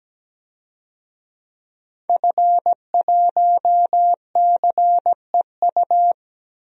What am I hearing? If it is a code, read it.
Morse code, heard as a keyed tone, about 17 wpm, 700 Hz. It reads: F1CEU
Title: DXCC-100 Call Signs - 5 Characters - 17wpm